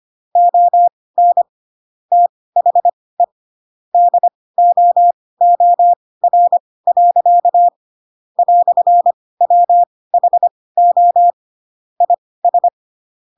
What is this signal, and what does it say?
Morse code, heard as a keyed tone, about 25 wpm, spaced as for 18 wpm, 700 Hz. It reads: ON THE DOOR. "WHO IS
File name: Pinocchio / Pinocchio ch2 12 25 ik1hge